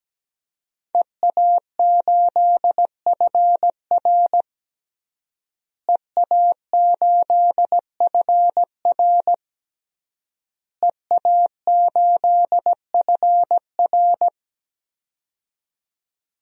Morse code, heard as a keyed tone, about 17 wpm, 700 Hz. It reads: EA8FR EA8FR EA8FR
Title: DXCC-100 Call Signs - 5 Characters - Repeated 3x - 17wpm